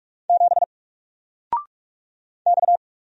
Morse code, heard as a keyed tone, about 44 wpm, 700 Hz. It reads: 7 X
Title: Single Character (Letter:Number:Punctuation:Prosign) Alt 45wpm